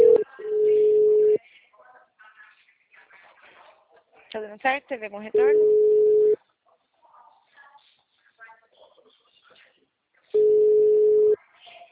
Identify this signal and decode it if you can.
Ringback tone (Europe, 425 Hz), with speech.
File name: out- -2017- - - .30663